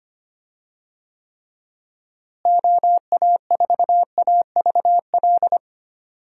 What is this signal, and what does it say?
Morse code, heard as a keyed tone, about 25 wpm, 700 Hz. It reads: OA4AVL
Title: USA Call Signs - 6 Characters - Repeated 3x - 25wpm